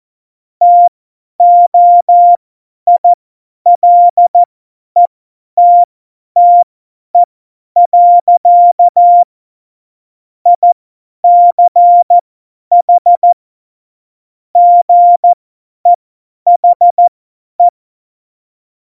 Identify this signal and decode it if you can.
Morse code, heard as a keyed tone, about 14 wpm, spaced as for 10 wpm, 700 Hz. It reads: TOILETTE. ICH GEHE
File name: Morsekurs / Witze / Witz 28. November 2017, 50 F70